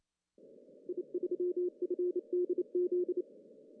Morse code, heard as a keyed tone, about 28 wpm, 340 Hz. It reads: I3FDZ
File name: CW / cqww-cw-k5tr